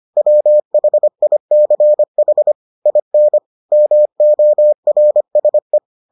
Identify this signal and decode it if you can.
Morse code, heard as a keyed tone, about 25 wpm, 600 Hz. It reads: WHICH IN MORSE